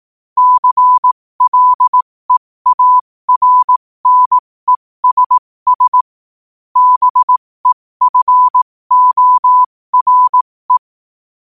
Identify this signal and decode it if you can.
Morse code, heard as a keyed tone, about 18 wpm, spaced as for 15 wpm, 1000 Hz. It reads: CLEARNESS BEFORE
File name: qotdfiles / 15wpm